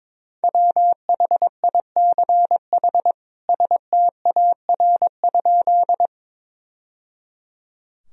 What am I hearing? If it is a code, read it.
Morse code, heard as a keyed tone, about 22 wpm, 700 Hz. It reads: WHICH STAR?
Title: Sentences from Top 1000 Words - Review Part 2 - 22wpm